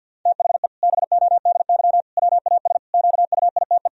Morse code, heard as a keyed tone, about 49 wpm, 700 Hz. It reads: THE BODY WAS QUITE
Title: Sentences from Top 1000 Words - Review Part 2 - 50wpm